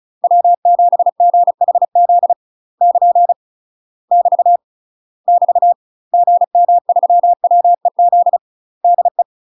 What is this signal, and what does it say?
Morse code, heard as a keyed tone, about 34 wpm, 700 Hz. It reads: W7GHZ <KN> <BT> <BT> GM3WEZ DE
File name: UK MASTER QSOs TRAINING / 35WPM / UK MASTER QSOs 0004